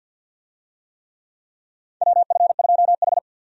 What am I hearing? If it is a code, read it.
Morse code, heard as a keyed tone, about 50 wpm, 700 Hz. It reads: WF2H
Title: USA Call Signs - 4 Characters - 50wpm